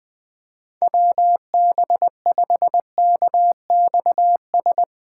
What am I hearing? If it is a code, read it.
Morse code, heard as a keyed tone, about 20 wpm, 700 Hz. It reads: WB5KXS
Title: Speed Racing - USA Call Signs 17wpm